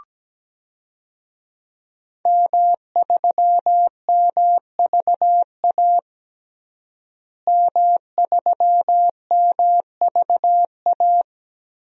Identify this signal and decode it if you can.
Morse code, heard as a keyed tone, about 17 wpm, 700 Hz. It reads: M3MVA M3MVA
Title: DXCC-100 Call Signs - 5 Characters - Repeated 2x - 17wpm